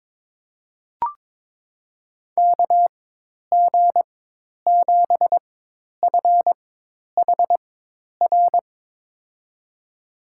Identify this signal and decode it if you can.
Morse code, heard as a keyed tone, about 22 wpm, spaced as for 10 wpm, 700 Hz. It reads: KG7FHR
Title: USA Call Signs - 6 Characters - 4x Character Spacing - 22wpm